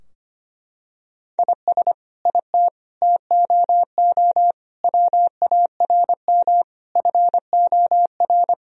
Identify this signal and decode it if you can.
Morse code, heard as a keyed tone, about 25 wpm, 700 Hz. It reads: IS IT TOO WARM FOR
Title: Sentences from Top 1000 Words - Review Part 2 - 25wpm